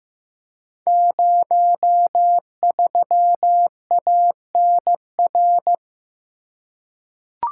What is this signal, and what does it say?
Morse code, heard as a keyed tone, about 15 wpm, 700 Hz. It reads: 03ANR
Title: Code Groups - 5 Letter:Numbers - 15wpm